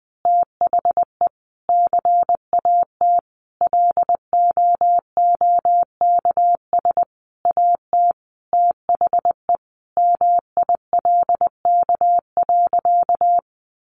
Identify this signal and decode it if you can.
Morse code, heard as a keyed tone, about 20 wpm, 700 Hz. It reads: THE CAT LOOKS AT THE MILK.
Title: Additional Easy Phrases 20wpm